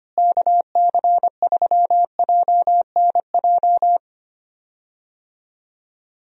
Morse code, heard as a keyed tone, about 25 wpm, 700 Hz. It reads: KC3JNJ
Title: USA Call Signs - 6 Characters - Repeated 2x - 25wpm